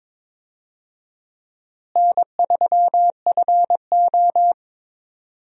Morse code, heard as a keyed tone, about 22 wpm, 700 Hz. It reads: N3FO
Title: USA Call Signs - 4 Characters - Repeated 2x - 22wpm